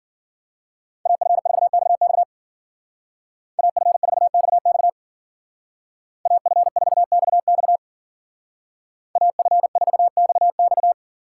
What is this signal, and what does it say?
Morse code, heard as a keyed tone, about 48 wpm, 700 Hz. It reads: AF4XX AF4XX AF4XX AF4XX
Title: Speed Racing - USA 2x2 Call Signs 40wpm